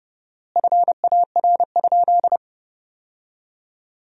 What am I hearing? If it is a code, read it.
Morse code, heard as a keyed tone, about 29 wpm, 700 Hz. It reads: FAR?
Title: 4x Word-Spacing - Sentences from Top 300 Words - 30wpm